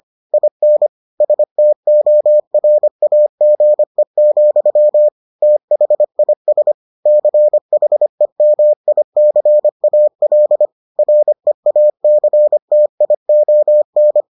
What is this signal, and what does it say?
Morse code, heard as a keyed tone, about 25 wpm, 600 Hz. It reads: IN STORAGE, THIS CHEMICAL REACTION